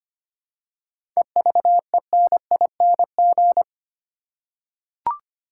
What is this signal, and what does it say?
Morse code, heard as a keyed tone, about 25 wpm, 700 Hz. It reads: EVENING
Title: Seven Letter Words - 25wpm